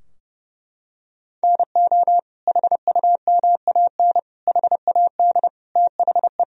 Morse code, heard as a keyed tone, about 32 wpm, 700 Hz. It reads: NO HUMAN HAD THE
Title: Sentences from Top 1000 Words - Review Part 3 - 30wpm